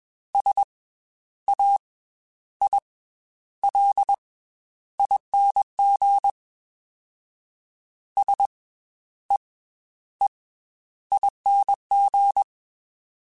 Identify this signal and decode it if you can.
Morse code, heard as a keyed tone, about 21 wpm, 800 Hz. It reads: S A I L ING S E E ING
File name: Special INGs / Special INGs-mp3s / Slow ing @20